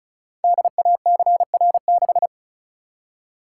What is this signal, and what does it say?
Morse code, heard as a keyed tone, about 36 wpm, 690 Hz. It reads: DACR6
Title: Code Groups - 5 Letter:Numbers - 35wpm